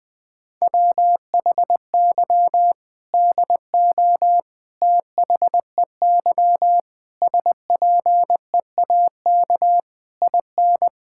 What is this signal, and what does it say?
Morse code, heard as a keyed tone, about 20 wpm, 700 Hz. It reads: WHY DO THEY SPEAK IN
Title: Sentences from Top 1600 Words 20wpm